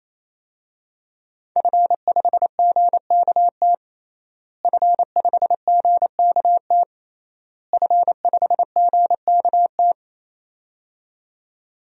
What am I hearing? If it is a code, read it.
Morse code, heard as a keyed tone, about 27 wpm, 700 Hz. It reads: F5GKT F5GKT F5GKT
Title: DXCC-100 Call Signs - 5 Characters - Repeated 3x - 28wpm